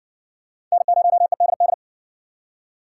Morse code, heard as a keyed tone, about 55 wpm, 700 Hz. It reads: N0EDD